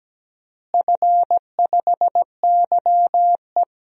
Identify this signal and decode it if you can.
Morse code, heard as a keyed tone, about 17 wpm, 700 Hz. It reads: F5YE
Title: DXCC-100 Call Signs - 4 Characters - Repeated 3x - 17wpm